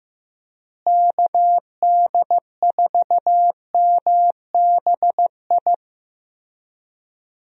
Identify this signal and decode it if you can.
Morse code, heard as a keyed tone, about 15 wpm, 700 Hz. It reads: KD4MBI
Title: USA Call Signs - 6 Characters - Repeated 2x - 15wpm